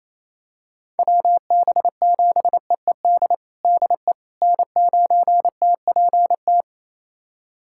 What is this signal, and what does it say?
Morse code, heard as a keyed tone, about 28 wpm, 700 Hz. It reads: WB7EED DE N9TPT
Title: USA Callsign DE Callsign 28wpm